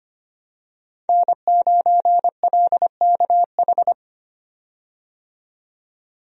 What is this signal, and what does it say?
Morse code, heard as a keyed tone, about 25 wpm, 700 Hz. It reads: N9LKH
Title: USA Call Signs - 5 Characters - 25wpm